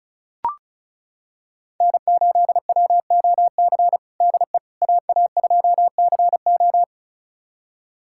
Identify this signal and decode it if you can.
Morse code, heard as a keyed tone, about 36 wpm, 700 Hz. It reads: N8WOC DE AA2CO
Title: USA Callsign DE Callsign - 5 Characters - 35wpm